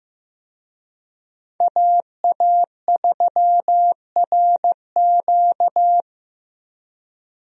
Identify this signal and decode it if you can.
Morse code, heard as a keyed tone, about 15 wpm, 700 Hz. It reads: AA3RQ
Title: USA Call Signs - 5 Characters - 15wpm